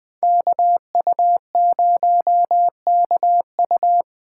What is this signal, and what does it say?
Morse code, heard as a keyed tone, about 20 wpm, 700 Hz. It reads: KU0KU